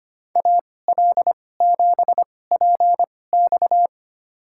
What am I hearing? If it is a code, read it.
Morse code, heard as a keyed tone, about 25 wpm, spaced as for 18 wpm, 700 Hz. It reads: AL7PX
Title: USA Call Signs - 2x Character Spacing - 25wpm